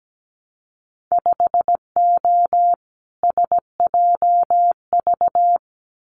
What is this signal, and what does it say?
Morse code, heard as a keyed tone, about 17 wpm, 700 Hz. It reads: 5O SJV
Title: Winter Field Day Exchanges Encore 17wpm